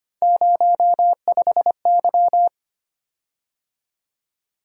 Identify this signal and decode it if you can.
Morse code, heard as a keyed tone, about 25 wpm, 700 Hz. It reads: 05Y
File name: Three Letter-Number Combinations - 25wpm